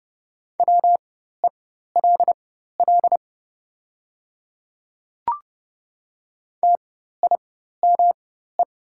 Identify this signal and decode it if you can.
Morse code, heard as a keyed tone, about 30 wpm, spaced as for 14 wpm, 700 Hz. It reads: WELL TIME
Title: 4x Character Spacing - Top 200 Words - 30wpm